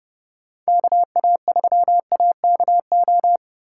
Morse code, heard as a keyed tone, about 32 wpm, 700 Hz. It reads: KA3AKO